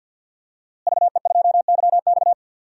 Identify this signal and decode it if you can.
Morse code, heard as a keyed tone, about 51 wpm, 700 Hz. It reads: VE2YX
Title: DXCC-100 Call Signs - 5 Characters - Repeated 2x - 50wpm